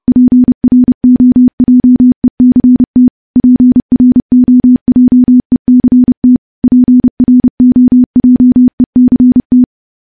Morse code, heard as a keyed tone, about 30 wpm, 250 Hz. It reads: PROJECT PROJECT PROJECT
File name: morse / morse-training-30WPM